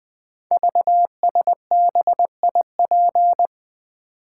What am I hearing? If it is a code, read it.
Morse code, heard as a keyed tone, about 20 wpm, 700 Hz. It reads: VSBIP
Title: Code Groups - 5 Letter:Numbers - 20wpm